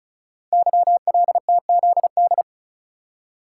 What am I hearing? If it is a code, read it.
Morse code, heard as a keyed tone, about 36 wpm, 700 Hz. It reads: YLTZD
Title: Code Groups - 5 Letter:Numbers - 35wpm